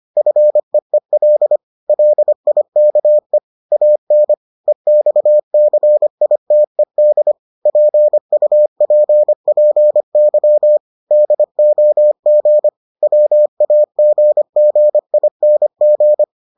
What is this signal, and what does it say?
Morse code, heard as a keyed tone, about 25 wpm, 600 Hz. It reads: FEEL LIKE AN EXCITED PUPPY DOG WAGGING